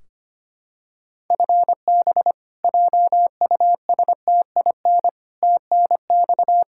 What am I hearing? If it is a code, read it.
Morse code, heard as a keyed tone, about 25 wpm, 700 Hz. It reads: FB JUSTIN TNX
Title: Practice QSO - Exchange 3 - More Info - 25wpm